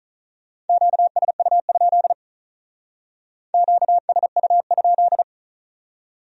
Morse code, heard as a keyed tone, about 44 wpm, 700 Hz. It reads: QSU? QSU?